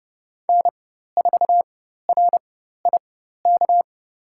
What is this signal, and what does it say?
Morse code, heard as a keyed tone, about 30 wpm, spaced as for 14 wpm, 700 Hz. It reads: N4RIK
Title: USA Call Signs - 5 Characters - 4x Character Spacing - 30wpm